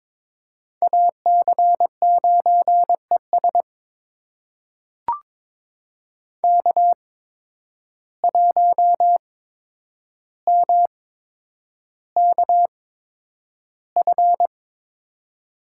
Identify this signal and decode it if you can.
Morse code, heard as a keyed tone, about 22 wpm, spaced as for 9 wpm, 700 Hz. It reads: AC9ES K1MKF
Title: USA Call Signs - 5 Characters - 8x Character Spacing - 22wpm